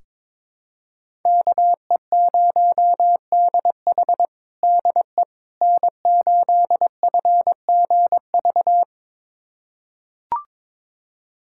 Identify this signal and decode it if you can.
Morse code, heard as a keyed tone, about 22 wpm, 700 Hz. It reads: KE0DH DE N8FGV
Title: USA Callsign DE Callsign - 5 Characters - 22wpm